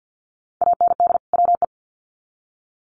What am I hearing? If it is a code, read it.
Morse code, heard as a keyed tone, about 50 wpm, 700 Hz. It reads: AND WE
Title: Sets of 2 Words 50wpm